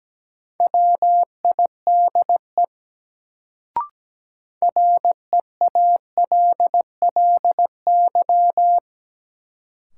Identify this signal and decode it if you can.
Morse code, heard as a keyed tone, about 17 wpm, 700 Hz. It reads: WIDE REALLY
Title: Sets of 4 Words - New Word Drill - 17wpm